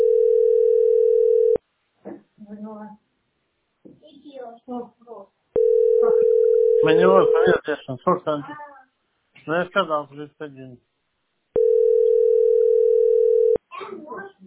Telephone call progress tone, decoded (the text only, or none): ringback tone